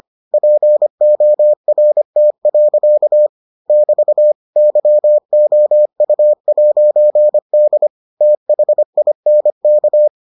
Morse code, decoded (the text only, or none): PORT. <BT> YOU'D THINK